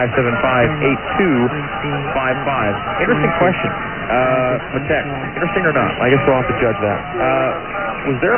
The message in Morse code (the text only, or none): none